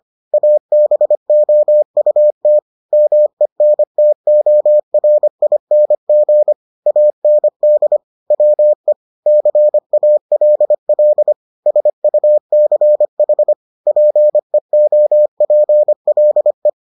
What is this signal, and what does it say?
Morse code, heard as a keyed tone, about 25 wpm, 600 Hz. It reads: ABOUT MENTORING AND WE CALL SUCH PEOPLE